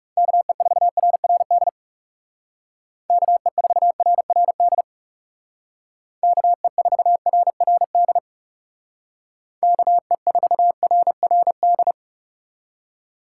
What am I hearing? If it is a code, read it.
Morse code, heard as a keyed tone, about 39 wpm, 700 Hz. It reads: KE4RRD KE4RRD KE4RRD KE4RRD